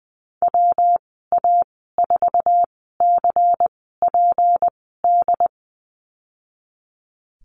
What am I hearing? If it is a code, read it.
Morse code, heard as a keyed tone, about 20 wpm, spaced as for 15 wpm, 700 Hz. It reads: WA4CPD
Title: 219 Lesson 35g - Number 0 - Call Signs 20wpm